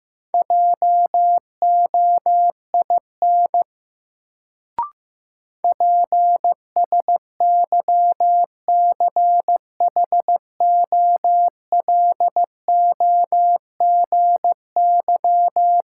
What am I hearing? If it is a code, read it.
Morse code, heard as a keyed tone, about 15 wpm, 700 Hz. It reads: JOIN PSYCHOLOGY
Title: Sentences from Top 1400 Words - New Word Drill Alt - 15wpm